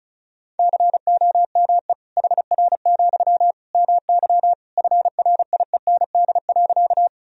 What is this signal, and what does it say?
Morse code, heard as a keyed tone, about 36 wpm, 700 Hz. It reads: COME HR, MY FRIEND.
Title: Sentences from Top 300 Words and Abbreviations 35wpm